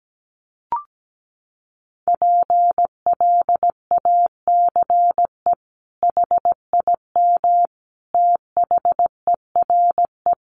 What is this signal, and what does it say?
Morse code, heard as a keyed tone, about 17 wpm, 700 Hz. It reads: PLACE HIM THERE
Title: Sentences from Top 1500 Words 17wpm